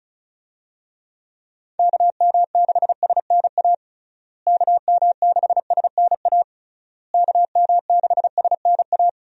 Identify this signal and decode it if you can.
Morse code, heard as a keyed tone, about 36 wpm, 700 Hz. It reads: KM6SNA KM6SNA KM6SNA
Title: USA Call Signs - 6 Characters - Repeated 3x - 35wpm